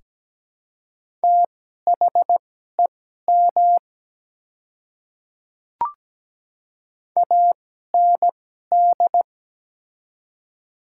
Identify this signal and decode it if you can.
Morse code, heard as a keyed tone, about 17 wpm, spaced as for 12 wpm, 700 Hz. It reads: THEM AND
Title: 2x Character Spacing - Top 200 Words - 17wpm